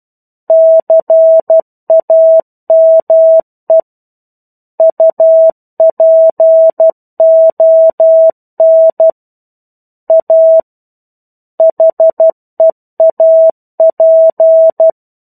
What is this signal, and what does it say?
Morse code, heard as a keyed tone, about 12 wpm, 650 Hz. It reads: CAME UPON A HEAP
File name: 01-Chapter-01 12 WPM